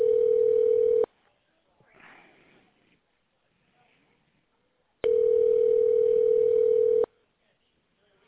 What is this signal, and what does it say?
Ringback tone (North America), 440 + 480 Hz.